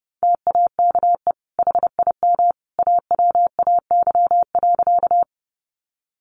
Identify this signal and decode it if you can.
Morse code, heard as a keyed tone, about 28 wpm, 700 Hz. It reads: TAKE HIM AWAY.
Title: Sentences from Top 600 Words 30wpm